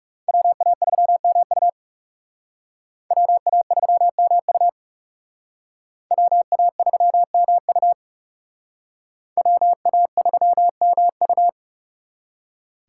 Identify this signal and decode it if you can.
Morse code, heard as a keyed tone, about 35 wpm, 700 Hz. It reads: WA3MU WA3MU WA3MU WA3MU
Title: Speed Racing - USA 2x2 Call Signs 30wpm